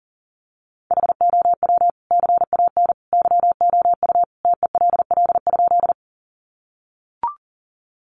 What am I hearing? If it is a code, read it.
Morse code, heard as a keyed tone, about 40 wpm, 700 Hz. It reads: HOW CAN YOU TELL?
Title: Sentences from Top 200 Words 40wpm